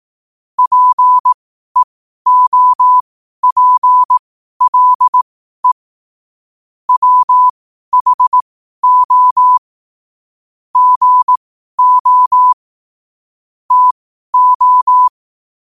Morse code, heard as a keyed tone, about 18 wpm, spaced as for 12 wpm, 1000 Hz. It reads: PEOPLE WHO GO TO
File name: qotdfiles / 12wpm